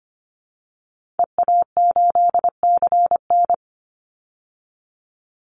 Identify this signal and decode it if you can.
Morse code, heard as a keyed tone, about 25 wpm, 700 Hz. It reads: EA8CN
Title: Call Signs Repeated 2x - 25wpm